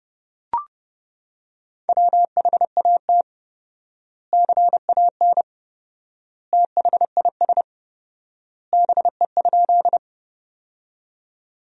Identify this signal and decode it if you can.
Morse code, heard as a keyed tone, about 30 wpm, 700 Hz. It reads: WHAT CAN THIS BE?
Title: 4x Word-Spacing - Sentences from Top 100 Words - 30wpm